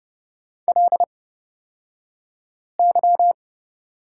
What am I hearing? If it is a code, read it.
Morse code, heard as a keyed tone, about 30 wpm, spaced as for 5 wpm, 700 Hz. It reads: LY
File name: Lesson 38d - Letter q - Two Characters 30wpm